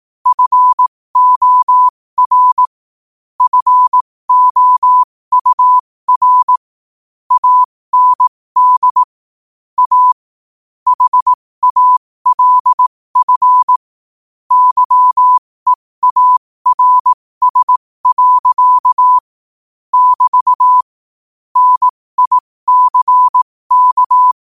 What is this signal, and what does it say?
Morse code, heard as a keyed tone, about 18 wpm, spaced as for 15 wpm, 1000 Hz. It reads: FOR FOUR AND A HALF YEARS. <BT> NICK
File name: qotdfiles / 15wpm